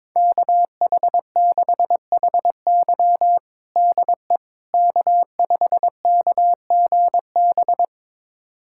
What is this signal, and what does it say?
Morse code, heard as a keyed tone, about 21 wpm, 700 Hz. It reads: KH6HY DE K5KGB
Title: USA Callsign DE Callsign - 5 Characters - 22wpm